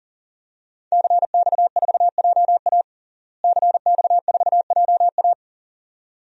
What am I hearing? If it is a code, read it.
Morse code, heard as a keyed tone, about 40 wpm, 700 Hz. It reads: CX4JA CX4JA
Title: DXCC-100 Call Signs - 5 Characters - Repeated 2x - 40wpm